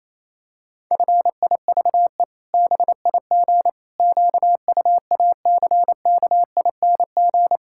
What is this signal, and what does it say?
Morse code, heard as a keyed tone, about 28 wpm, 700 Hz. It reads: FIVE BIG QUACKING
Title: Pangram Challenge v2 - 28wpm